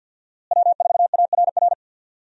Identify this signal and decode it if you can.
Morse code, heard as a keyed tone, about 52 wpm, 700 Hz. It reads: W4ARR
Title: USA Call Signs - 5 Characters - Repeated 3x - 50wpm